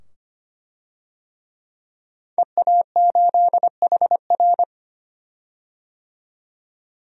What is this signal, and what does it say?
Morse code, heard as a keyed tone, about 25 wpm, 700 Hz. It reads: EA8HR
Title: DXCC-100 Call Signs - 5 Characters - 25wpm